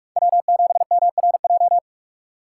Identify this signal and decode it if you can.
Morse code, heard as a keyed tone, about 58 wpm, 700 Hz. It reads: W7MRJ